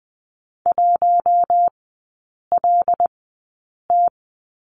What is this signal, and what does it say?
Morse code, heard as a keyed tone, about 20 wpm, spaced as for 8 wpm, 700 Hz. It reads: 1LT
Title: 074 Lesson 13e - Letter L - Three Characters 20wpm